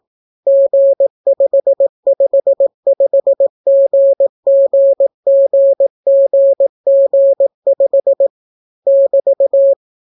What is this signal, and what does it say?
Morse code, heard as a keyed tone, about 18 wpm, 550 Hz. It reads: G555GGGGG5 <BT>